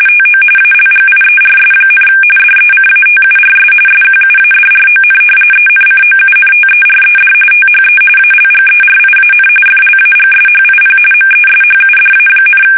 RTTY, about 45.45 baud. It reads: TO ALGIERS, DECATURS SQUADRON ENCOUNTERED THE ALGERIAN FLAGSHIP MESHUDA,
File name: fsk850 / usnavy2